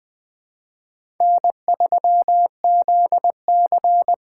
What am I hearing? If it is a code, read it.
Morse code, heard as a keyed tone, about 20 wpm, 700 Hz. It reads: N3ZC
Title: USA Call Signs - 4 Characters - 2x Character Spacing - 20wpm